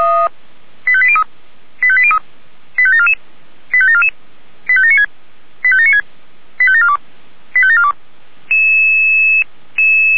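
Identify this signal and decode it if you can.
Five-tone selective call (ZVEI), heard as quick runs of five tones, tones of about 70 ms. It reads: 85803 85803 85744 85744 85796 85796 85742 85742
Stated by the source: Five-tone ZVEI selective call received on an SDR.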